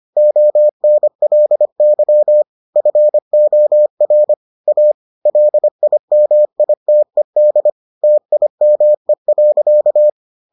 Morse code, heard as a keyed tone, about 25 wpm, 600 Hz. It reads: ONLY FOR A LIMITED TIME.